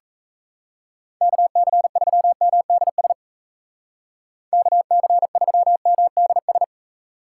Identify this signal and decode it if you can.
Morse code, heard as a keyed tone, about 35 wpm, 700 Hz. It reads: KC3MDS KC3MDS
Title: Speed Racing - USA 2x3 Call Signs 28wpm